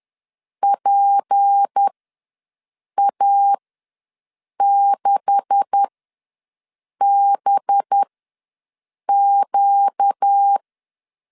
Morse code, heard as a keyed tone, about 10 wpm, spaced as for 6 wpm, 800 Hz. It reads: PA6BQ
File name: LEZIONI / 9 2867 ws 10wpm 4215